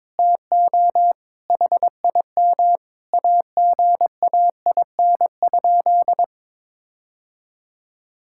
Morse code, heard as a keyed tone, about 21 wpm, 700 Hz. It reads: TO HIM AGAIN?
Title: Sentences from Top 1600 Words 22wpm